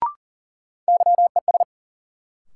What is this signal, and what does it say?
Morse code, heard as a keyed tone, about 40 wpm, 700 Hz. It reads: YES
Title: Sentences from Top 500 Words - New Word Drill - 40wpm